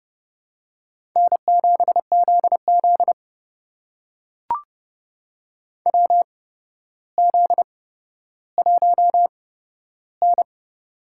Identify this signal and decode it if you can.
Morse code, heard as a keyed tone, about 31 wpm, spaced as for 13 wpm, 700 Hz. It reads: N7ZZ WZ1N
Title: USA Call Signs - 4 Characters - 8x Character Spacing - 30wpm